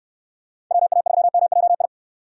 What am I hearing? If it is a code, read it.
Morse code, heard as a keyed tone, about 64 wpm, 700 Hz. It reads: WA3MJI